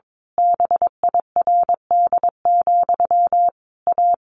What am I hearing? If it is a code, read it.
Morse code, heard as a keyed tone, about 22 wpm, 700 Hz. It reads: BIRD, A
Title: A Sound Of Thunder 22wpm